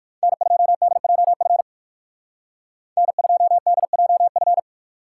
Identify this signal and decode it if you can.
Morse code, heard as a keyed tone, about 58 wpm, 700 Hz. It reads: N2DJF N2DJF